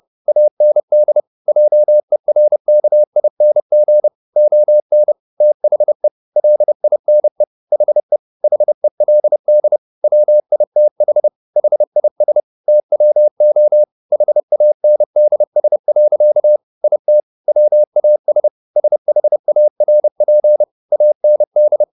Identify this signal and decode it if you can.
Morse code, heard as a keyed tone, about 30 wpm, 600 Hz. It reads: AND JERKING ON THE LINE HE HELD WITH HIS TWO HANDS. IT WAS SHARP AND